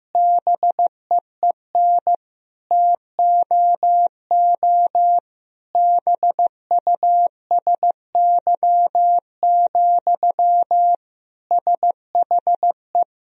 Morse code, heard as a keyed tone, about 16 wpm, 700 Hz. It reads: BEEN TOO BUSY, SHE
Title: Sentences from Top 1000 Words - Review Part 3 - 15wpm